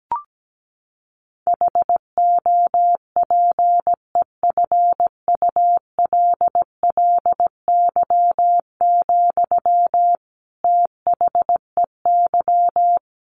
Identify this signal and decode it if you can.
Morse code, heard as a keyed tone, about 17 wpm, 700 Hz. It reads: HOPEFULLY, THEY
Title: Sentences from Top 1500 Words 17wpm